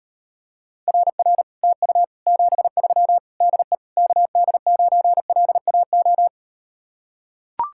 Morse code, heard as a keyed tone, about 38 wpm, 700 Hz. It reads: RR TU 73 DE KD9LAO